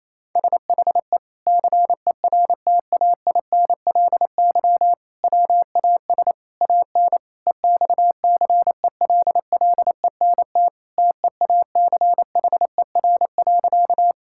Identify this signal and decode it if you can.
Morse code, heard as a keyed tone, about 28 wpm, 700 Hz. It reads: SHE CERTAINLY WAS AN EXCELLENT TEACHER.
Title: Sentences from Top 1000 Words - Review Part 6 - 28wpm